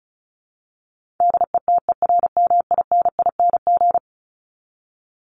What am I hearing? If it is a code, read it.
Morse code, heard as a keyed tone, about 34 wpm, 700 Hz. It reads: DETERMINING
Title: ING Suffix - Top 100 Words with Base Words 35wpm